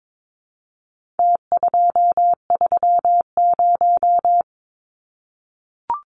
Morse code, heard as a keyed tone, about 22 wpm, 700 Hz. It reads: T230
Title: Numbers - Contest Style - 4-Digit 22wpm